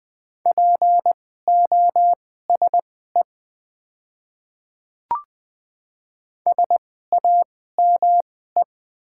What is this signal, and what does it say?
Morse code, heard as a keyed tone, about 20 wpm, spaced as for 14 wpm, 700 Hz. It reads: POSE SAME
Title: 2x Character Spacing - Top 500 Words - 20wpm